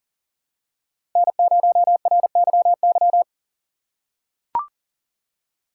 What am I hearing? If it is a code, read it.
Morse code, heard as a keyed tone, about 40 wpm, 700 Hz. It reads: N0RYY
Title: USA Call Signs - 4x Character Spacing - 40wpm